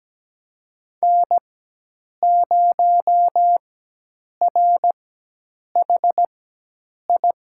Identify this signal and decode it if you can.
Morse code, heard as a keyed tone, about 17 wpm, spaced as for 8 wpm, 700 Hz. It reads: N0RHI
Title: USA Call Signs - 5 Characters - 4x Character Spacing - 17wpm